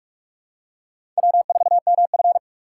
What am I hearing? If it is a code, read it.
Morse code, heard as a keyed tone, about 46 wpm, 700 Hz. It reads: W4MF
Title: USA Call Signs - 6x Character Spacing - 45wpm